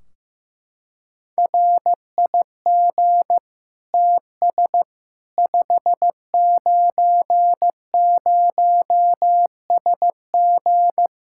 Morse code, heard as a keyed tone, about 15 wpm, 700 Hz. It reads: RIG TS 590SG
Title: Practice QSO - Exchange 2 - Info - 15wpm